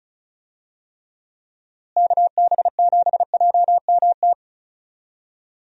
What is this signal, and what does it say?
Morse code, heard as a keyed tone, about 36 wpm, 700 Hz. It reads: KB7JMT
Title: USA Call Signs - 6 Characters - Repeated 3x - 35wpm